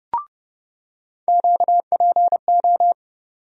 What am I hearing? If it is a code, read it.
Morse code, heard as a keyed tone, about 30 wpm, 700 Hz. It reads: QPO
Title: USA Call Sign Postfixes - 3 Letters 30wpm